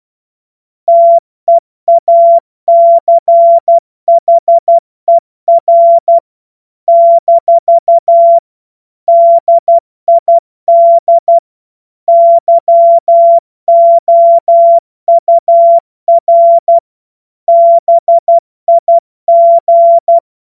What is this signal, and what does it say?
Morse code, heard as a keyed tone, about 12 wpm, 680 Hz. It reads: TEACHER - DID YOUR BIG
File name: Morsekurs / Witze